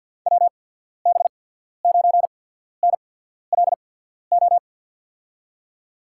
Morse code, heard as a keyed tone, about 53 wpm, spaced as for 14 wpm, 700 Hz. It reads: WB9NLO
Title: USA Call Signs - 6 Characters - 8x Character Spacing - 50wpm